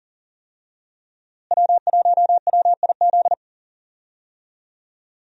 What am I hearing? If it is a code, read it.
Morse code, heard as a keyed tone, about 39 wpm, 700 Hz. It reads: W1WIZ
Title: USA Call Signs - 5 Characters - Repeated 3x - 40wpm